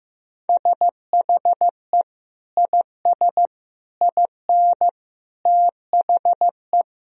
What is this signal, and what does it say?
Morse code, heard as a keyed tone, about 15 wpm, 700 Hz. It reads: SHE IS IN THE
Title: Sentences from Top 1000 Words - Review Part 4 - 15wpm